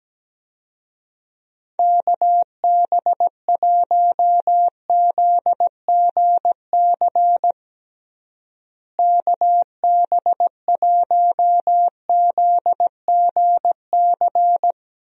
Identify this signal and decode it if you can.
Morse code, heard as a keyed tone, about 17 wpm, 700 Hz. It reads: KB1ZGC KB1ZGC
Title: USA Call Signs - 6 Characters - Repeated 2x - 17wpm